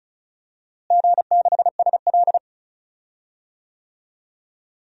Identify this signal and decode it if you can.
Morse code, heard as a keyed tone, about 36 wpm, 700 Hz. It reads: G6SL
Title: DXCC-100 Call Signs - 4 Characters - Repeated 2x - 35wpm